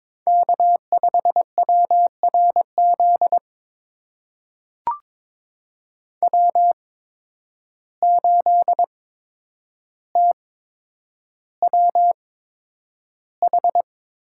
Morse code, heard as a keyed tone, about 22 wpm, spaced as for 9 wpm, 700 Hz. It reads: K5WRZ W8TWH
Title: USA Call Signs - 5 Characters - 8x Character Spacing - 22wpm